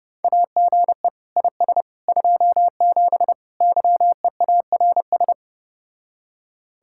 Morse code, heard as a keyed tone, about 29 wpm, 700 Hz. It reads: AGE IS 27 YEARS